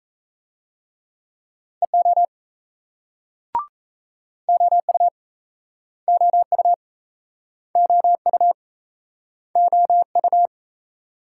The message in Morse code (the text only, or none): EO OU OU OU OU